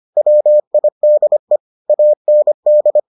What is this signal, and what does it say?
Morse code, heard as a keyed tone, about 25 wpm, 600 Hz. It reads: WIDE AND